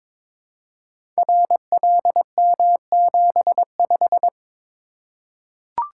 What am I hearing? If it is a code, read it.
Morse code, heard as a keyed tone, about 22 wpm, 700 Hz. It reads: RLM75
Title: Code Groups - 5 Letter:Numbers - 22wpm